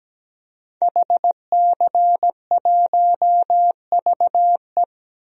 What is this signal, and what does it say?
Morse code, heard as a keyed tone, about 17 wpm, 700 Hz. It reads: HC1VE
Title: DXCC-100 Call Signs - 5 Characters - Repeated 2x - 17wpm